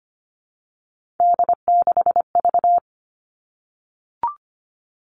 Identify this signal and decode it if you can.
Morse code, heard as a keyed tone, about 25 wpm, 700 Hz. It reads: D6V
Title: Three Challenging Letter-Number Combinations 25wpm